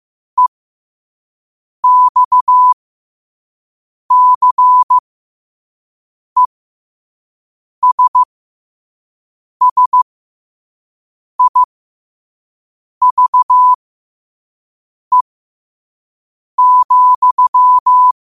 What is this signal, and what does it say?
Morse code, heard as a keyed tone, about 15 wpm, spaced as for 5 wpm, 1000 Hz. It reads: EXCESSIVE,
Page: New quotes every day in morse code at 5 Words per minute.